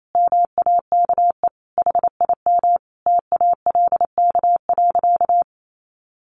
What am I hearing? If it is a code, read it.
Morse code, heard as a keyed tone, about 28 wpm, 700 Hz. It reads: MAKE HIM TALK.
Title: Sentences from Top 600 Words 28wpm